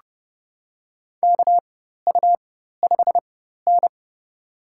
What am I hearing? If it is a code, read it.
Morse code, heard as a keyed tone, about 30 wpm, spaced as for 14 wpm, 700 Hz. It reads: KU5N
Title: USA Call Signs - 4x Character Spacing - 30wpm